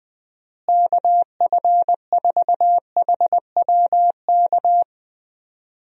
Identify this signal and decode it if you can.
Morse code, heard as a keyed tone, about 20 wpm, 700 Hz. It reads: KF4HWK
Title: USA Call Signs - 6 Characters - Repeated 2x - 20wpm